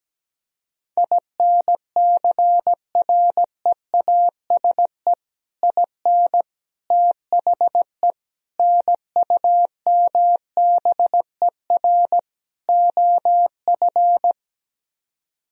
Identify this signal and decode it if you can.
Morse code, heard as a keyed tone, about 17 wpm, 700 Hz. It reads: INCREASE IN THE NUMBER OF
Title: Speed-Racing - Sets of 5 Words 17wpm